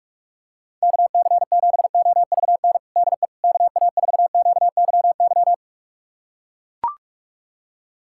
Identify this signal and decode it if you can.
Morse code, heard as a keyed tone, about 41 wpm, 700 Hz. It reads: KC7OVN DE KA4QYY
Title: USA Callsign DE Callsign 45wpm